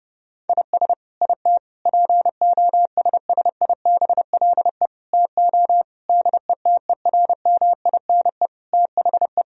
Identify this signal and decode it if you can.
Morse code, heard as a keyed tone, about 28 wpm, 700 Hz. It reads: IS IT POSSIBLE TO DETERMINE THE
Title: Sentences from Top 1000 Words - Review Part 4 - 30wpm